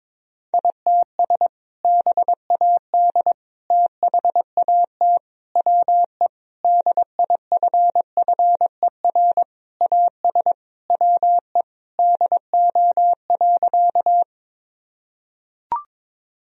Morse code, heard as a keyed tone, about 22 wpm, 700 Hz. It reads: ITS BAD THAT WE DIFFER AS WE DO.
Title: Sentences from Top 1000 Words - Review Part 1 - 22wpm